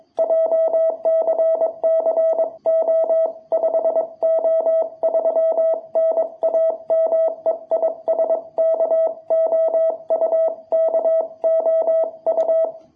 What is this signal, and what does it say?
Morse code, heard as a keyed tone, about 22 wpm, 660 Hz. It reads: JCCO5O3 NAME IS KOU KOU
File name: jh8cbh / 2021cw